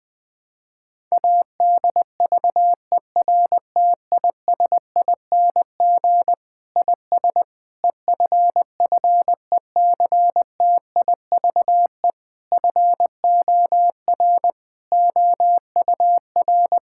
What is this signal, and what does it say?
Morse code, heard as a keyed tone, about 21 wpm, 700 Hz. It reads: ADVERTISING IS EFFECTIVE FOR OUR
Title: Sentences from Top 1000 Words - Review Part 6 - 20wpm